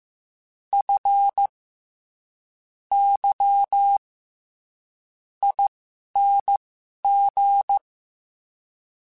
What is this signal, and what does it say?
Morse code, heard as a keyed tone, about 15 wpm, spaced as for 11 wpm, 800 Hz. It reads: F Y ING
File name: Special INGs-mp3s / Special INGs-mp3s / SlowToFaster ing @10